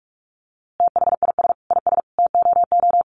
Morse code, heard as a keyed tone, about 46 wpm, 700 Hz. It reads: THIS IS TOO
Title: A Sound Of Thunder 45wpm